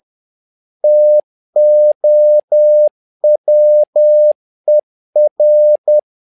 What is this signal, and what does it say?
Morse code, heard as a keyed tone, about 10 wpm, 600 Hz. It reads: TOWER
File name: learning center / tower trouble / tower trouble 10wpm